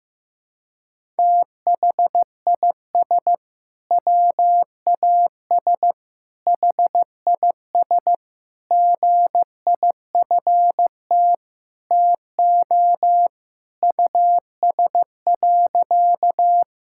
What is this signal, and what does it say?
Morse code, heard as a keyed tone, about 15 wpm, 700 Hz. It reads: THIS WAS HIS GIFT TO US.
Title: Sentences from Top 1000 Words - Review Part 6 - 15wpm